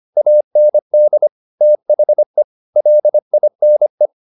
Morse code, heard as a keyed tone, about 25 wpm, 600 Hz. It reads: AND THE LINE